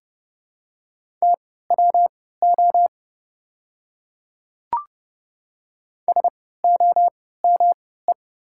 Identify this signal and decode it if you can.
Morse code, heard as a keyed tone, about 30 wpm, spaced as for 17 wpm, 700 Hz. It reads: TWO SOME
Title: 3x Character Spacing - Top 100 Words - 30wpm